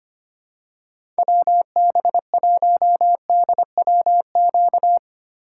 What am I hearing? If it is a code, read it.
Morse code, heard as a keyed tone, about 25 wpm, 700 Hz. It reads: WB1DWQ
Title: USA Call Signs - 6 Characters - 2x Character Spacing - 25wpm